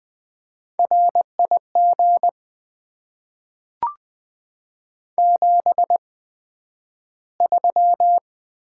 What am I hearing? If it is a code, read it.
Morse code, heard as a keyed tone, about 20 wpm, spaced as for 11 wpm, 700 Hz. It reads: RIG 73
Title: Top 20 QSO Elements - 8x Character Spacing - 20wpm